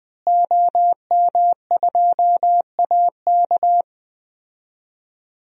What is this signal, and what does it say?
Morse code, heard as a keyed tone, about 20 wpm, 700 Hz. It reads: OM2AK
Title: DXCC-100 Call Signs - 5 Characters - Repeated 2x - 20wpm